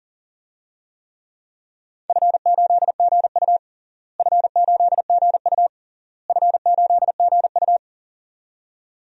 Morse code, heard as a keyed tone, about 40 wpm, 700 Hz. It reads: F8GU F8GU F8GU
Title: DXCC-100 Call Signs - 4 Characters - Repeated 3x - 40wpm